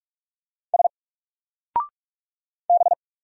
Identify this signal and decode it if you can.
Morse code, heard as a keyed tone, about 44 wpm, 700 Hz. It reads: S B
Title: Single Character (Letter:Number:Punctuation:Prosign) 45wpm